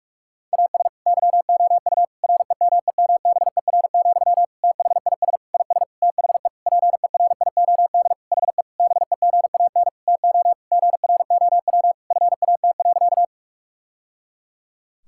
Morse code, heard as a keyed tone, about 46 wpm, 700 Hz. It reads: AS YOU REMEMBER, THIS IS THE PERIOD HE BEGAN TO GROW FAT.
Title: Sentences from Top 1000 Words - Review Part 5 - 45wpm